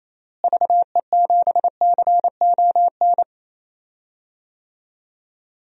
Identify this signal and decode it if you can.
Morse code, heard as a keyed tone, about 27 wpm, 700 Hz. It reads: VE7CON
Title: DXCC-100 Call Signs - 6 Characters - Repeated 2x - 28wpm